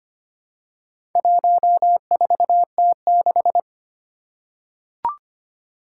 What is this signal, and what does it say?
Morse code, heard as a keyed tone, about 25 wpm, 700 Hz. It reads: 14T6
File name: practice / 25wpm / Four Letter-Number Combinations - 25wpm